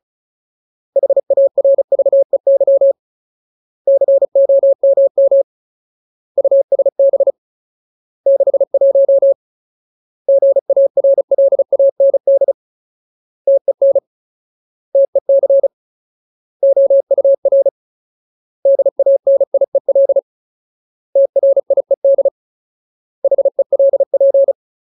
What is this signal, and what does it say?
Morse code, heard as a keyed tone, about 34 wpm, 550 Hz. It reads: HARVEY COMM USB 61 GARLAND TEN TEC OUR DANIEL TRIED HELP
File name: Ultimate Word List / 35WPM Ultimate